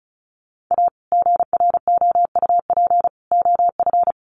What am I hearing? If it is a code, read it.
Morse code, heard as a keyed tone, about 34 wpm, 700 Hz. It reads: A GROUP OF
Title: Sets of 3 Words 35wpm